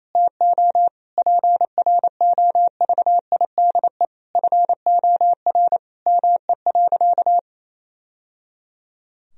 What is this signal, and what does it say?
Morse code, heard as a keyed tone, about 28 wpm, 700 Hz. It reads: TO PROVIDE FOR ME.
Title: Sentences from Top 1000 Words - Review Part 3 - 28wpm